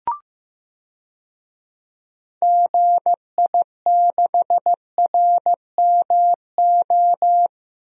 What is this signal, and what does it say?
Morse code, heard as a keyed tone, about 15 wpm, 700 Hz. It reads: GI6RMO
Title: DXCC-100 Call Signs - 6 Characters - 15wpm